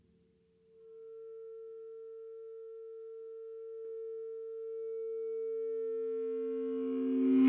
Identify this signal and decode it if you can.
Dial tone (China, 450 Hz).